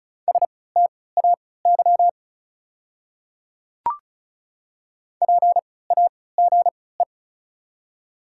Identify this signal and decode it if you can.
Morse code, heard as a keyed tone, about 36 wpm, spaced as for 20 wpm, 690 Hz. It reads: STAY PAGE
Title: 3x Character Spacing - Top 500 Words - 35wpm